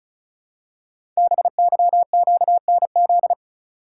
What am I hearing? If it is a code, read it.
Morse code, heard as a keyed tone, about 36 wpm, 700 Hz. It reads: BYQNZ
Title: Code Groups - 5 Letters - 35wpm